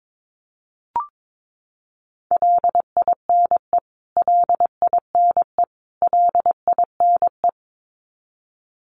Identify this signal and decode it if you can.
Morse code, heard as a keyed tone, about 22 wpm, 700 Hz. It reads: LINE LINE LINE
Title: Top 500 Words Repeated 22wpm